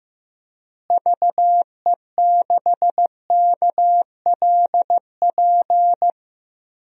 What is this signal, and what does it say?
Morse code, heard as a keyed tone, about 15 wpm, 700 Hz. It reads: VE6KLP
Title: DXCC-100 Call Signs - 6 Characters - Repeated 2x - 15wpm